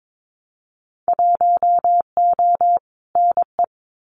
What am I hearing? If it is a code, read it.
Morse code, heard as a keyed tone, about 22 wpm, 700 Hz. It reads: 1O NE
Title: Winter Field Day Exchanges Encore 22wpm